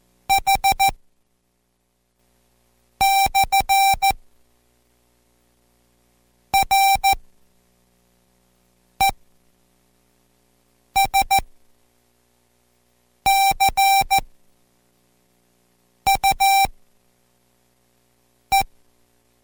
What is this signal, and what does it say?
Morse code, heard as a keyed tone, about 14 wpm, spaced as for 4 wpm, 780 Hz. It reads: H/RESCUE